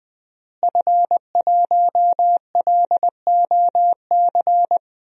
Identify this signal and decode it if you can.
Morse code, heard as a keyed tone, about 21 wpm, 700 Hz. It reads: F1LOC
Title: DXCC-100 Call Signs - 5 Characters - Repeated 3x - 20wpm